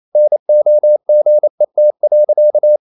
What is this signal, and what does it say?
Morse code, heard as a keyed tone, about 28 wpm, 600 Hz. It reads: NOGET.
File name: Ferie QRQ 28wpm